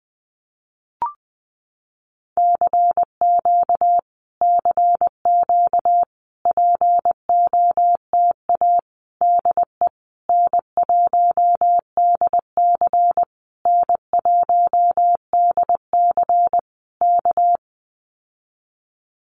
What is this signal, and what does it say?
Morse code, heard as a keyed tone, about 20 wpm, 700 Hz. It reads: CQ CQ POTA DE N1DC N1DC K
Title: Common QSO Phrases Encore 20wpm